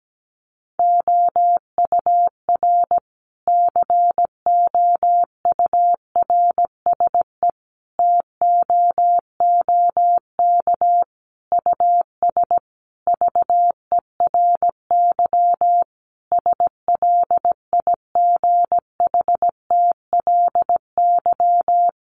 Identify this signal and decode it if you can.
Morse code, heard as a keyed tone, about 17 wpm, 700 Hz. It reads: OUR COURSE TOOK US VERY SLIGHTLY
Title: Sentences from Top 1300 Words 17wpm